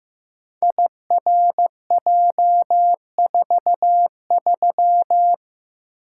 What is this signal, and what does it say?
Morse code, heard as a keyed tone, about 15 wpm, 700 Hz. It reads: IRJ43
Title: Code Groups - 5 Letter:Numbers - 15wpm